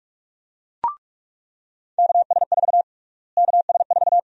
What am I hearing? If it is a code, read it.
Morse code, heard as a keyed tone, about 43 wpm, 700 Hz. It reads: KS4 KS4
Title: USA Call Sign Prefixes - Repeated 2x - 45wpm